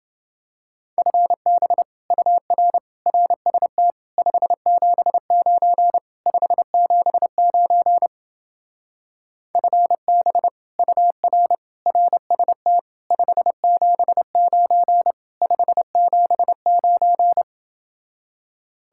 Morse code, heard as a keyed tone, about 30 wpm, 700 Hz. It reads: FB UR RST 579 579 FB UR RST 579 579